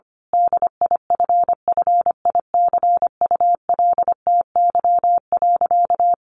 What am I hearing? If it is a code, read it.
Morse code, heard as a keyed tone, about 25 wpm, 700 Hz. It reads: DIFFICULTY.
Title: Sentences from Top 1500 Words 25wpm